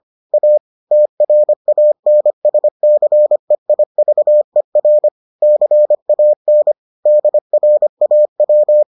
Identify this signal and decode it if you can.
Morse code, heard as a keyed tone, about 25 wpm, 600 Hz. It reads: A TRANSCEIVER CAN DRAW